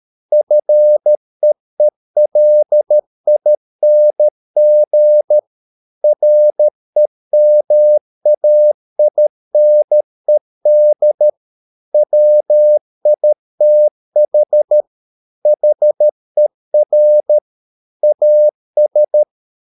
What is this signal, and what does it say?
Morse code, heard as a keyed tone, about 13 wpm, 600 Hz. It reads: FEELING REMAINED WITH HER AS